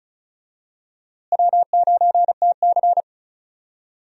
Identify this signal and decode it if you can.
Morse code, heard as a keyed tone, about 34 wpm, 700 Hz. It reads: W9TC
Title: USA Call Signs - 4 Characters - Repeated 2x - 35wpm